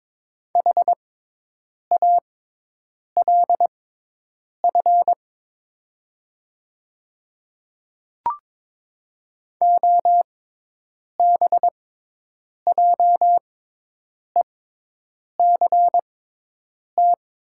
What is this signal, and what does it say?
Morse code, heard as a keyed tone, about 22 wpm, spaced as for 8 wpm, 700 Hz. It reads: HALF OBJECT
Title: 6x Character Spacing - Top 500 Words - 22wpm